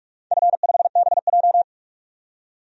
Morse code, heard as a keyed tone, about 46 wpm, 700 Hz. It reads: F5BJ
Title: DXCC-100 Call Signs - 4 Characters - 45wpm